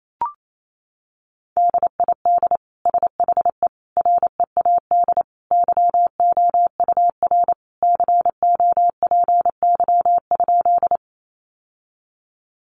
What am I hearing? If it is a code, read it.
Morse code, heard as a keyed tone, about 28 wpm, 700 Hz. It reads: DID SHE READ YOUR COPY?
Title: Sentences from Top 1400 Words 28wpm